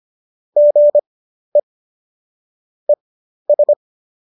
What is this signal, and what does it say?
Morse code, heard as a keyed tone, about 25 wpm, spaced as for 12 wpm, 590 Hz. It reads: GE ES